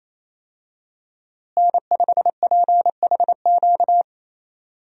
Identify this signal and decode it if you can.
Morse code, heard as a keyed tone, about 28 wpm, 700 Hz. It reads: N5PHQ
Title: USA Call Signs - 5 Characters - Repeated 2x - 28wpm